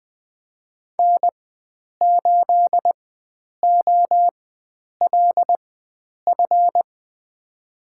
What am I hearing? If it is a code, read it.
Morse code, heard as a keyed tone, about 20 wpm, spaced as for 9 wpm, 700 Hz. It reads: N8OLF
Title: USA Call Signs - 5 Characters - 4x Character Spacing - 20wpm